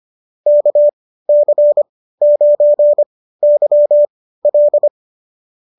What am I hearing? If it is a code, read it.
Morse code, heard as a keyed tone, about 25 wpm, spaced as for 15 wpm, 590 Hz. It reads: KC9YL